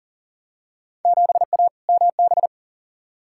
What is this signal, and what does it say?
Morse code, heard as a keyed tone, about 40 wpm, 700 Hz. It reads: 7A MB
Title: Field Day Exchanges Encore - 40wpm